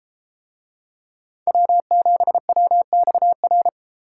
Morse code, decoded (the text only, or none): W7WXR